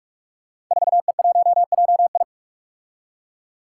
Morse code, heard as a keyed tone, about 44 wpm, 700 Hz. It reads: 4E1JI
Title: Five Challenging Letter-Number Combinations 45wpm